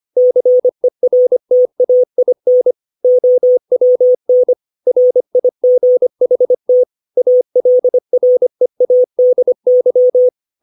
Morse code, heard as a keyed tone, about 25 wpm, 500 Hz. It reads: CERTAIN OWN RIGHT ALREADY
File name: 632 words / 632 Words 25WPM